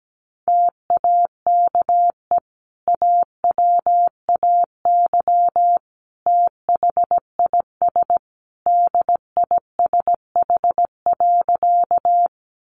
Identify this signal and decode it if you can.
Morse code, heard as a keyed tone, about 17 wpm, 700 Hz. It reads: TAKE AWAY THIS DISH.
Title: Sentences from Top 1300 Words 17wpm